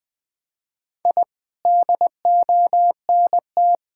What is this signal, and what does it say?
Morse code, heard as a keyed tone, about 20 wpm, 700 Hz. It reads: I DONT
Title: Sentences with 7-Word Phrases - Spoken Beginning - 20wpm